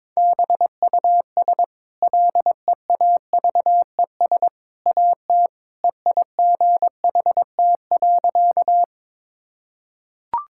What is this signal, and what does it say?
Morse code, heard as a keyed tone, about 22 wpm, 690 Hz. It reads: BUS LEAVES AT EIGHT.
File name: practice / 22wpm / Easy Sentences with Hints - 22wpm